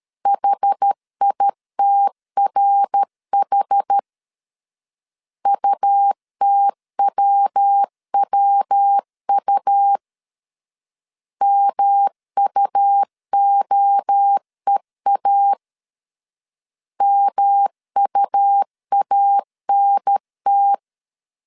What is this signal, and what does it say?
Morse code, heard as a keyed tone, about 13 wpm, 800 Hz. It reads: HITRH UTWWU MUOEA MUANT
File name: CW / DET / 1 3 1500C S B 12wpm 2828